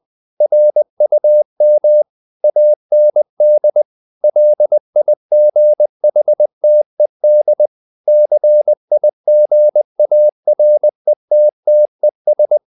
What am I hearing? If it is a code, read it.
Morse code, heard as a keyed tone, about 20 wpm, 600 Hz. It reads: RUM AND LIGHTED CIGARETTES
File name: the old man and the sea part 2 20wpm